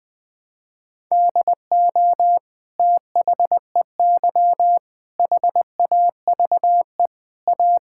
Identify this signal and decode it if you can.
Morse code, heard as a keyed tone, about 21 wpm, 700 Hz. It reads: DO THEY HAVE A
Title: Sentences from Top 1000 Words - Review Part 5 - 20wpm